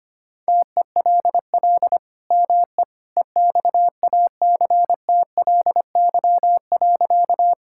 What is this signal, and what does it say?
Morse code, heard as a keyed tone, about 25 wpm, 700 Hz. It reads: TELL ME EXACTLY.
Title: Sentences from Top 1000 Words - Review Part 6 - 25wpm